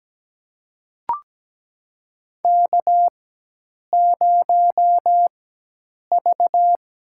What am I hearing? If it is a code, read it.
Morse code, heard as a keyed tone, about 17 wpm, spaced as for 8 wpm, 700 Hz. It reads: K0V
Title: USA Call Signs - 4x Character Spacing - 17wpm